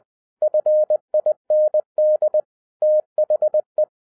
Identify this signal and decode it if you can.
Morse code, heard as a keyed tone, about 20 wpm, 610 Hz. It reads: FIND THE